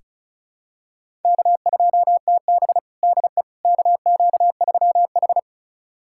Morse code, heard as a keyed tone, about 34 wpm, 700 Hz. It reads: K2TB DE KQ3H
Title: USA Callsign DE Callsign - 4 Characters - 35wpm